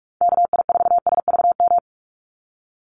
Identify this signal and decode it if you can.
Morse code, heard as a keyed tone, about 44 wpm, 700 Hz. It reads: KI4SVM
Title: NA SOTA Chaser Callsigns 45wpm